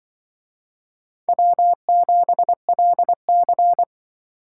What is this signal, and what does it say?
Morse code, heard as a keyed tone, about 25 wpm, 700 Hz. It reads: W7LC
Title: Speed Racing - USA 1x2 Call Signs 20wpm